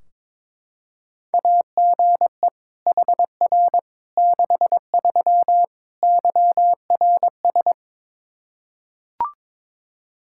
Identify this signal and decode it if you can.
Morse code, heard as a keyed tone, about 22 wpm, 700 Hz. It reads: AGE HR 63 YRS
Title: Practice QSO - Exchange 3 - More Info - 22wpm